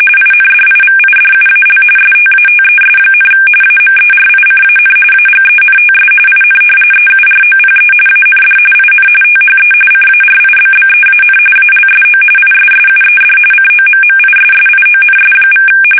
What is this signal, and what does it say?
RTTY, about 45.45 baud. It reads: INDIVIDUAL STATES, AMERICAN AGENTS IN EUROPE AND IN THE CARIBBEAN ALSO ISSUED COMMISSIONS=
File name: fsk850 / usnavy1